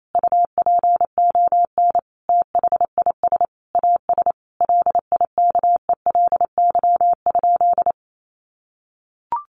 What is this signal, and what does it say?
Morse code, heard as a keyed tone, about 28 wpm, 700 Hz. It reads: UPON THIS AS LIKELY?
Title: Sentences from Top 1100 Words 28wpm